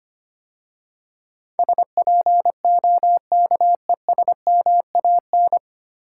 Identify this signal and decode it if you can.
Morse code, heard as a keyed tone, about 25 wpm, 700 Hz. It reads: SPOKESMAN
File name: practice / 25wpm / 9 Letter Words - First 3 Letters Given - 25wpm